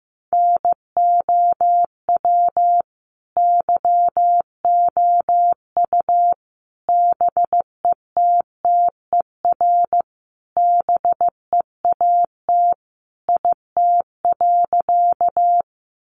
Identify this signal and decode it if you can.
Morse code, heard as a keyed tone, about 15 wpm, 700 Hz. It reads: NOW YOU BETTER BEAT IT.
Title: Sentences from Top 1300 Words 15wpm